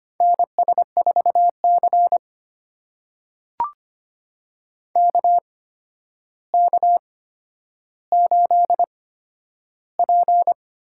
Morse code, heard as a keyed tone, about 25 wpm, spaced as for 11 wpm, 700 Hz. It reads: NS4C KK8P
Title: USA Call Signs - 4 Characters - 8x Character Spacing - 25wpm